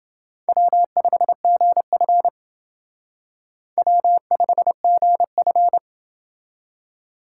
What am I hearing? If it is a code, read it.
Morse code, heard as a keyed tone, about 28 wpm, 700 Hz. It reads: W5GF W5GF